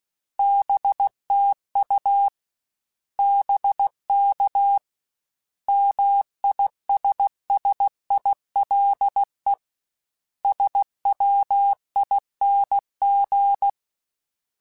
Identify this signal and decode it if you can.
Morse code, heard as a keyed tone, about 16 wpm, 800 Hz. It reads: BTU BK MISSILE SWING
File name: Short Runs / Short Runs-mp3s / Words @15